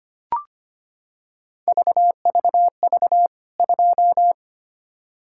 Morse code, heard as a keyed tone, about 25 wpm, 700 Hz. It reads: VVV 2
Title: K1USN SST QSO Practice 25wpm